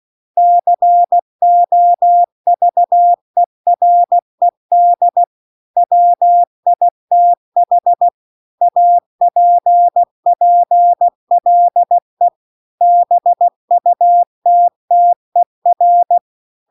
Morse code, heard as a keyed tone, about 16 wpm, 700 Hz. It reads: COVERED WITH APPLE BUTTER
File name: CW-The-Big-Twoheaded-River-2-16WPM